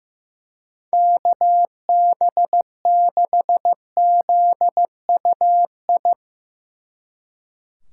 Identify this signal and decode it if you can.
Morse code, heard as a keyed tone, about 15 wpm, 700 Hz. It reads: KB6ZUI
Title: Speed Racing - USA Call Signs 15wpm